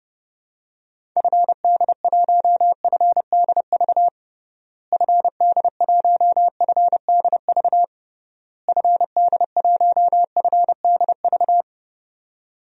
Morse code, heard as a keyed tone, about 30 wpm, 700 Hz. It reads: FD1FDV FD1FDV FD1FDV
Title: DXCC-100 Call Signs - 6 Characters - Repeated 3x - 30wpm